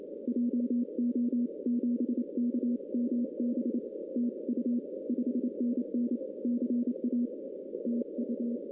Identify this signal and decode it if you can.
Morse code, heard as a keyed tone, about 27 wpm, 250 Hz. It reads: JO7KMB TU 5NN CA TU
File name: 2016 arrl dx cw / 2016-ARRL-DX-CW-K5TR- - 0-25